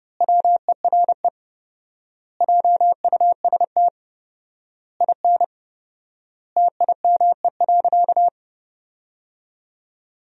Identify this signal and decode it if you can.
Morse code, heard as a keyed tone, about 30 wpm, 700 Hz. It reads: WERE JUST IN TIME.
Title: 4x Word-Spacing - Sentences from Top 300 Words - 30wpm